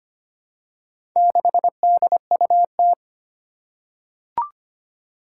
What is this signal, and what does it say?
Morse code, heard as a keyed tone, about 25 wpm, 700 Hz. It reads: 6DUT
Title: Four Challenging Letter-Number Combinations 25wpm